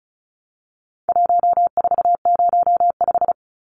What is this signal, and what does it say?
Morse code, heard as a keyed tone, about 34 wpm, 700 Hz. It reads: 1405
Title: Numbers - 4-Digit 35wpm